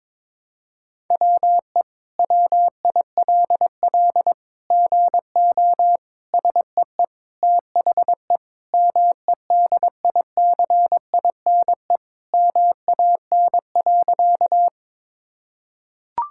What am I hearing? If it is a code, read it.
Morse code, heard as a keyed tone, about 22 wpm, 700 Hz. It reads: WE WILL GO SEE THE MEDICINE MAN.
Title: Sentences from Top 1000 Words - Review Part 6 - 22wpm